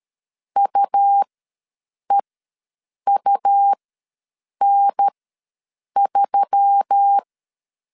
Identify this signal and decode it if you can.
Morse code, heard as a keyed tone, about 13 wpm, spaced as for 7 wpm, 800 Hz. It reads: UEUN3
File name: LEZIONI / 5 LBVC / 5 LB ws 12wpm 3034